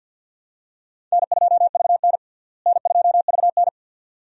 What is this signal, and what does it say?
Morse code, heard as a keyed tone, about 52 wpm, 700 Hz. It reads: N2VN N2VN
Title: USA Call Signs - 4 Characters - Repeated 2x - 50wpm